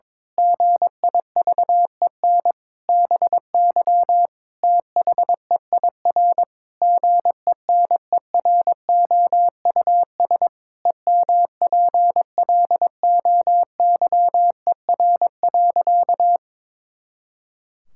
Morse code, heard as a keyed tone, about 22 wpm, 700 Hz. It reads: GIVEN BY THEIR GENEROUS EMPLOYER.
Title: Pangram Challenge v2 - 22wpm